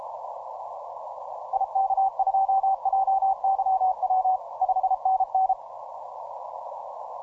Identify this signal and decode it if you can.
Morse code, heard as a keyed tone, about 31 wpm, 770 Hz, with radio noise. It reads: IK2JYW 5NN
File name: simple-file-list / 9X0T CW 10M